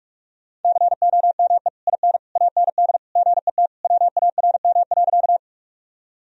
Morse code, heard as a keyed tone, about 46 wpm, 700 Hz. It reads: COME IN AND GET WARM.
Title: Sentences from Top 1000 Words - Review Part 2 - 45wpm